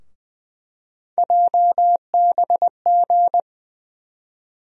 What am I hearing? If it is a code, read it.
Morse code, heard as a keyed tone, about 20 wpm, 700 Hz. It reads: JBG
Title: USA Call Sign Postfixes - 3 Letters 20wpm